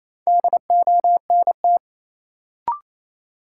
Morse code, heard as a keyed tone, about 28 wpm, 700 Hz. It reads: DONT
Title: Sentences from Top 300 Words - New Word Drill - 28wpm